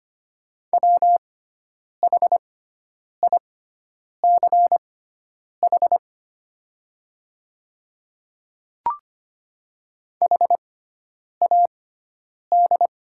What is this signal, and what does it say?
Morse code, heard as a keyed tone, about 25 wpm, spaced as for 9 wpm, 690 Hz. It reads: WHICH HAD
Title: 6x Character Spacing - Top 100 Words - 25wpm